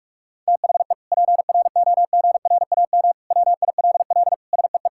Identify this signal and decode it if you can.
Morse code, heard as a keyed tone, about 42 wpm, 700 Hz. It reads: THE PROGRAM WILL SEE
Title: Sentences from Top 1000 Words - Review Part 5 - 45wpm